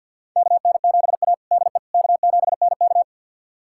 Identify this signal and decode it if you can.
Morse code, heard as a keyed tone, about 48 wpm, 700 Hz. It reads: KN7A DE K7NK
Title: USA Callsign DE Callsign - 4 Characters - 50wpm